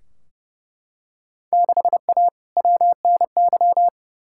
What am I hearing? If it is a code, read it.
Morse code, heard as a keyed tone, about 30 wpm, 700 Hz. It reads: 6A WNY
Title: Field Day Exchanges - 30wpm